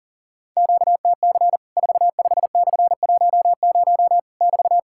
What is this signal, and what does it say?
Morse code, heard as a keyed tone, about 40 wpm, 700 Hz. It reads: QTC 45/10 <BT>
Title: Worked All Europe DX Contest - QTC Exchange - 40wpm